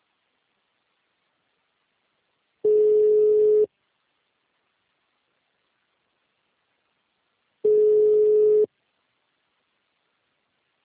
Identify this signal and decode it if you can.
Ringback tone (Europe, 425 Hz).